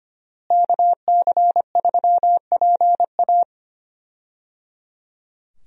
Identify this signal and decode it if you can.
Morse code, heard as a keyed tone, about 25 wpm, 700 Hz. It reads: KC3PA
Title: USA Call Signs - 5 Characters - Repeated 3x - 25wpm